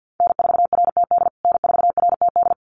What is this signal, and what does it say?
Morse code, heard as a keyed tone, about 50 wpm, 700 Hz. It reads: N4RTD N4RTD
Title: USA Call Signs Repeated 3x 50wpm